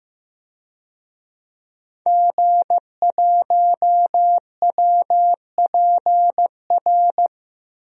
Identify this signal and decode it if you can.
Morse code, heard as a keyed tone, about 15 wpm, 700 Hz. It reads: G1WPR
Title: DXCC-100 Call Signs - 5 Characters - 15wpm